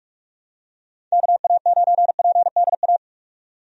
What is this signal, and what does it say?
Morse code, heard as a keyed tone, about 46 wpm, 700 Hz. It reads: KA9PDA
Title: USA Call Signs - 6 Characters - 45wpm